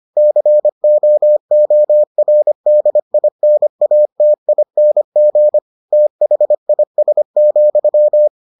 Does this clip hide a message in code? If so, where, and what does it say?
Yes: Morse code from 0.2 to 8.3 s, heard as a keyed tone, about 25 wpm, 600 Hz. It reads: COORDINATING THIS,